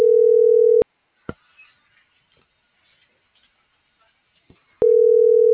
Ringback tone (North America), 440 + 480 Hz.
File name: CENTRAL GENERAL / out arturo.vega